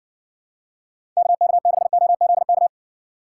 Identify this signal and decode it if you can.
Morse code, heard as a keyed tone, about 60 wpm, 700 Hz. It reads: KK6OZK